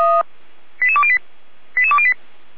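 Five-tone selective call (ZVEI), heard as quick runs of five tones, tones of about 70 ms. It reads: 88208 88208
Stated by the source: ZVEI five-tone selective call received on an SDR, from the feuerwehr-orlamuende fire brigade server.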